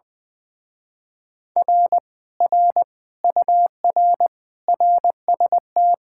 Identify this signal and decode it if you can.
Morse code, heard as a keyed tone, about 20 wpm, 700 Hz. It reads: R R UR RST
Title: Speed-Racing Common QSO Phrases - 20wpm